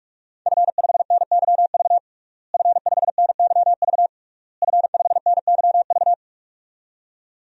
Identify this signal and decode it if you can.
Morse code, heard as a keyed tone, about 46 wpm, 700 Hz. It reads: F5NYV F5NYV F5NYV
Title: DXCC-100 Call Signs - 5 Characters - Repeated 3x - 45wpm